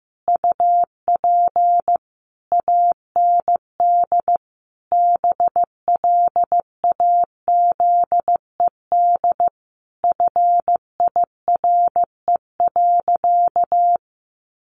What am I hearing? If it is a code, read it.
Morse code, heard as a keyed tone, about 15 wpm, 700 Hz. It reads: UP AND BLAZED FIRE.
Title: A Sound Of Thunder 15wpm